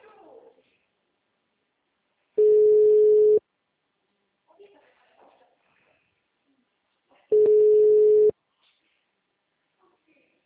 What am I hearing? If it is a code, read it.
Ringback tone (Europe, 425 Hz), with speech.